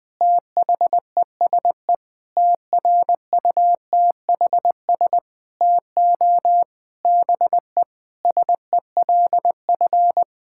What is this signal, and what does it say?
Morse code, heard as a keyed tone, about 20 wpm, 700 Hz. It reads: THESE TRUTHS TO BE SELF
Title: Sentences with 9-Word Phrases - Spoken Beginning - 20wpm